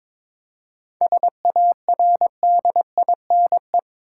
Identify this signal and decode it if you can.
Morse code, heard as a keyed tone, about 22 wpm, 700 Hz. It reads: SARDINE
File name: practice / 22wpm / Difficult Words with Hints - 22wpm